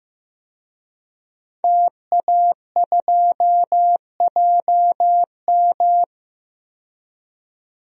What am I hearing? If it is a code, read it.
Morse code, heard as a keyed tone, about 15 wpm, 700 Hz. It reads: TA2JM
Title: DXCC-100 Call Signs - 5 Characters - 15wpm